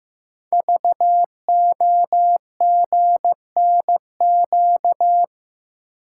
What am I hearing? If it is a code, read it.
Morse code, heard as a keyed tone, about 15 wpm, 700 Hz. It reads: VOGNQ
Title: Code Groups - 5 Letters - 15wpm